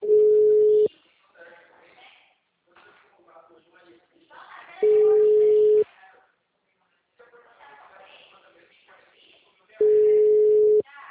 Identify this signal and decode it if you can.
Ringback tone (Europe, 425 Hz), with speech.